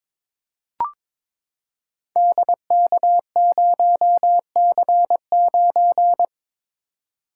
Morse code, heard as a keyed tone, about 22 wpm, 700 Hz. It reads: DK0C9
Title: Code Groups - 5 Letter:Numbers - 22wpm